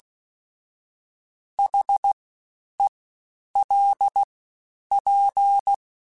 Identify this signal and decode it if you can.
Morse code, heard as a keyed tone, about 16 wpm, 800 Hz. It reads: H E L P
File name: Special INGs-mp3s / Special INGs-mp3s / Fast ing @15